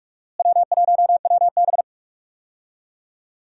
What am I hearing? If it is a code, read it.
Morse code, heard as a keyed tone, about 42 wpm, 700 Hz. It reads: W1WB
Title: USA Call Signs - 4 Characters - Repeated 3x - 45wpm